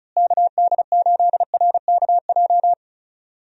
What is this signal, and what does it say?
Morse code, heard as a keyed tone, about 34 wpm, 700 Hz. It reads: KD8RKJ
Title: USA Call Signs - 6 Characters - Repeated 3x - 35wpm